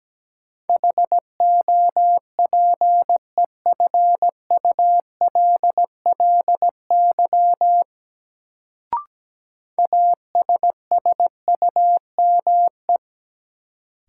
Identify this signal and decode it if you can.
Morse code, heard as a keyed tone, about 17 wpm, 700 Hz. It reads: HOPEFULLY ASSUME
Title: Sentences from Top 1500 Words - New Word Drill Alt - 17wpm